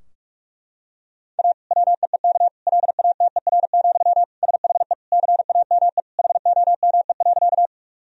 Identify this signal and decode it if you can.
Morse code, heard as a keyed tone, about 41 wpm, 700 Hz. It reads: A WEEK LATER, SHE CAME HOME.
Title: Sentences from Top 1000 Words - Review Part 4 - 45wpm